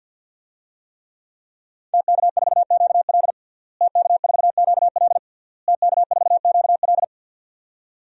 Morse code, heard as a keyed tone, about 52 wpm, 700 Hz. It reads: TK4QL TK4QL TK4QL
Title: DXCC-100 Call Signs - 5 Characters - Repeated 3x - 50wpm